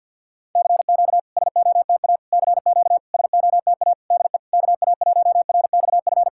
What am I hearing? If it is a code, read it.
Morse code, heard as a keyed tone, about 52 wpm, 690 Hz. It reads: CQ SOTA CQ SOTA DE KA1RXF
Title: SOTA QSO Practice 50wpm